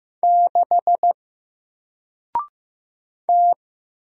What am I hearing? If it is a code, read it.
Morse code, heard as a keyed tone, about 15 wpm, 700 Hz. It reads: 6 T
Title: Single Letter-Number 15wpm